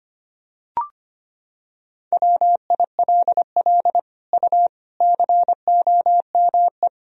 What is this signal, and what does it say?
Morse code, heard as a keyed tone, about 25 wpm, 700 Hz. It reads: WILL U COME
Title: Sentences from Top 200 Words and Abbreviations 25wpm